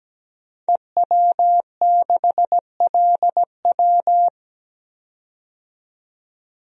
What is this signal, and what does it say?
Morse code, heard as a keyed tone, about 17 wpm, 700 Hz. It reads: EW6LW
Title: DXCC-100 Call Signs - 5 Characters - 17wpm